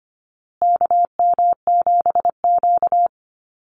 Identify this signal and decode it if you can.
Morse code, heard as a keyed tone, about 25 wpm, 700 Hz. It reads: KM7Q
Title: Call Signs Repeated 2x - 25wpm